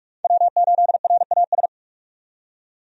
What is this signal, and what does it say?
Morse code, heard as a keyed tone, about 46 wpm, 700 Hz. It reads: W8RAS
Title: USA Call Signs - 5 Characters - 45wpm